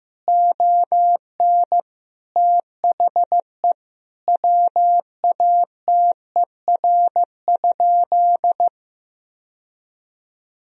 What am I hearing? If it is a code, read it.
Morse code, heard as a keyed tone, about 16 wpm, 700 Hz. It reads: ON THE WATER?
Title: Sentences from Top 1000 Words - Review Part 3 - 15wpm